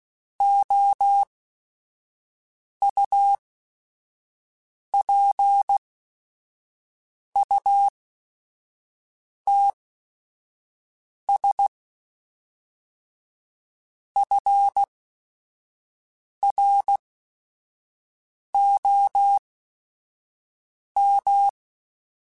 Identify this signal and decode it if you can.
Morse code, heard as a keyed tone, about 16 wpm, spaced as for 5 wpm, 800 Hz. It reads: OUPUTS FROM
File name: Technical Subjects / Technical Subjects-mp3s / Sw Pwr Sup @5